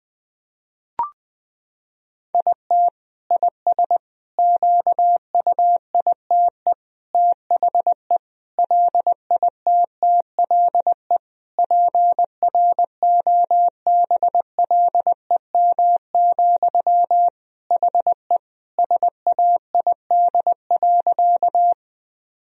Morse code, heard as a keyed tone, about 20 wpm, 700 Hz. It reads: IT IS QUITE THE LITTLE PROBLEM, HE SAID.
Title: Sentences from Top 1000 Words - Review Part 2 - 20wpm